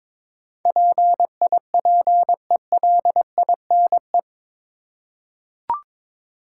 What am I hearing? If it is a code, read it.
Morse code, heard as a keyed tone, about 22 wpm, 700 Hz. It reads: PIPELINE
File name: practice / 22wpm / Eight Letter Words - 22wpm